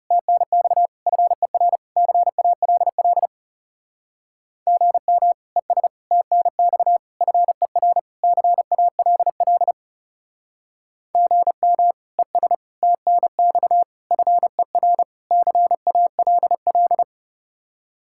Morse code, heard as a keyed tone, about 37 wpm, 700 Hz. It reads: TNX FER CALL GM ES TNX FER CALL GM ES TNX FER CALL
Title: Speed-Racing Common QSO Phrases - 30wpm